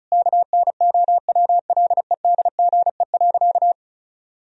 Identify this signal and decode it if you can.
Morse code, heard as a keyed tone, about 36 wpm, 700 Hz. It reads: KNOWLEDGE.
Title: Sentences from Top 1000 Words - Review Part 4 - 35wpm